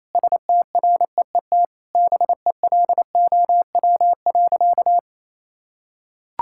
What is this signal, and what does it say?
Morse code, heard as a keyed tone, about 28 wpm, 700 Hz. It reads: STREET BELOW.
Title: Sentences from Top 1000 Words - Review Part 5 - 28wpm